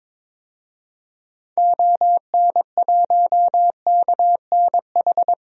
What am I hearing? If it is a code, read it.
Morse code, heard as a keyed tone, about 22 wpm, 700 Hz. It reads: ON1KNH
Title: DXCC-100 Call Signs - 6 Characters - 22wpm